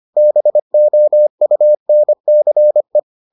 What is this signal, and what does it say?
Morse code, heard as a keyed tone, about 25 wpm, 600 Hz. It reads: BOUNCE